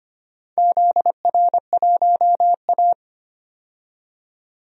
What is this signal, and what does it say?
Morse code, heard as a keyed tone, about 25 wpm, 700 Hz. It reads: ZR1A
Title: DXCC-100 Call Signs - 4 Characters - 25wpm